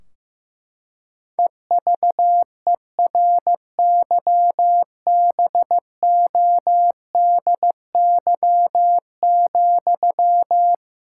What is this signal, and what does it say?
Morse code, heard as a keyed tone, about 15 wpm, 700 Hz. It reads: EVERYBODY,
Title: Sentences from Top 1600 Words 15wpm